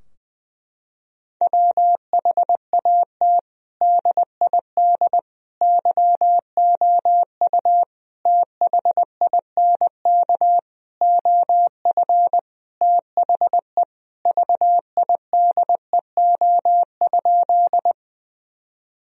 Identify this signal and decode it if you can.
Morse code, heard as a keyed tone, about 20 wpm, 700 Hz. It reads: WHAT DID YOU THINK OF THE VIDEO?
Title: Sentences from Top 1000 Words - Review Part 4 - 20wpm